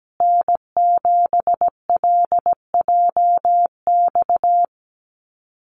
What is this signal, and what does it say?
Morse code, heard as a keyed tone, about 17 wpm, 700 Hz. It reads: N7LJX
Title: USA Call Signs Repeated 2x 17wpm